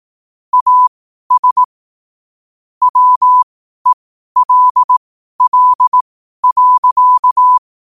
Morse code, heard as a keyed tone, about 18 wpm, spaced as for 12 wpm, 1000 Hz. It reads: AS WELL.
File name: qotdfiles / 12wpm